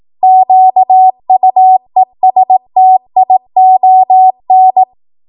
Morse code, heard as a keyed tone, about 18 wpm, 750 Hz. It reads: QUESTION